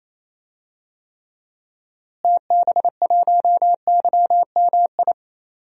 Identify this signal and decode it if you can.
Morse code, heard as a keyed tone, about 28 wpm, 700 Hz. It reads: TB1YMI
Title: DXCC-100 Call Signs - 6 Characters - Repeated 2x - 28wpm